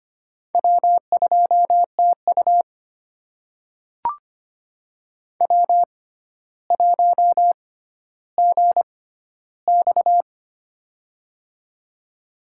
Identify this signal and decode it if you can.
Morse code, heard as a keyed tone, about 25 wpm, spaced as for 13 wpm, 700 Hz. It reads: W2TU W1GX
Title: USA Call Signs - 4 Characters - 6x Character Spacing - 25wpm